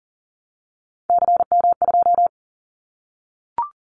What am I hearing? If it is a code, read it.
Morse code, heard as a keyed tone, about 40 wpm, 700 Hz. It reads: CM2
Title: Callsign Prefixes for Top 100 DXCC - 40wpm